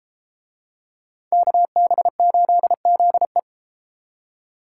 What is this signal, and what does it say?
Morse code, heard as a keyed tone, about 30 wpm, 700 Hz. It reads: KB8ZE